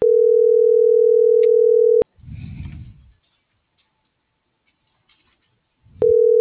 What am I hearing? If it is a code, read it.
Ringback tone (North America), 440 + 480 Hz.